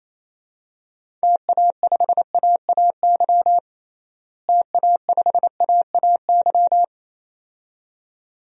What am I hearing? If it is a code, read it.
Morse code, heard as a keyed tone, about 28 wpm, 700 Hz. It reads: TA5AAY TA5AAY
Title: DXCC-100 Call Signs - 6 Characters - Repeated 2x - 28wpm